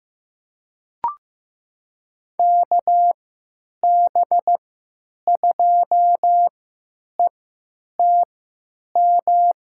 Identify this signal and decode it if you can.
Morse code, heard as a keyed tone, about 15 wpm, spaced as for 9 wpm, 700 Hz. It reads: KB2ETM
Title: USA Call Signs - 6 Characters - 3x Character Spacing - 15wpm